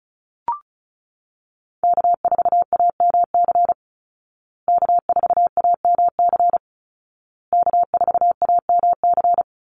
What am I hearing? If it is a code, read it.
Morse code, heard as a keyed tone, about 34 wpm, 700 Hz. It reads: K4AMC K4AMC K4AMC
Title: Call Signs Repeated 2x - 35wpm